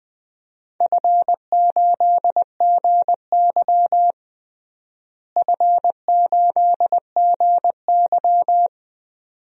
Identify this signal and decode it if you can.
Morse code, heard as a keyed tone, about 20 wpm, 700 Hz. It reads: F8GY F8GY
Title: DXCC-100 Call Signs - 4 Characters - Repeated 2x - 20wpm